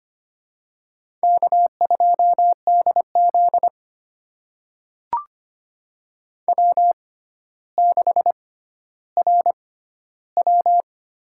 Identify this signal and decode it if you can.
Morse code, heard as a keyed tone, about 25 wpm, spaced as for 13 wpm, 700 Hz. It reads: K2DZ W6RW
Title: USA Call Signs - 4 Characters - 6x Character Spacing - 25wpm